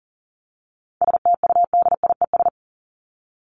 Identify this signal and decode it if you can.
Morse code, heard as a keyed tone, about 40 wpm, 700 Hz. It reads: STUDIES
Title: ES:S Suffix - Top 100 Words with Base Words 40wpm